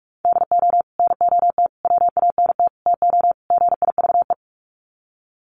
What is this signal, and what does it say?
Morse code, heard as a keyed tone, about 44 wpm, 700 Hz. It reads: DO NOT WANT TO GIVE
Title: Sets of 5 Words Encore 45wpm